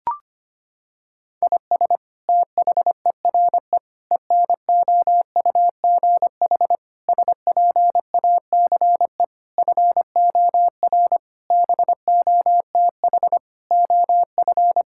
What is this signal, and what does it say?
Morse code, heard as a keyed tone, about 25 wpm, 700 Hz. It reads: IS THERE ENOUGH SPACE FOR BOTH OF
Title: Sentences from Top 1000 Words - Review Part 5 - 25wpm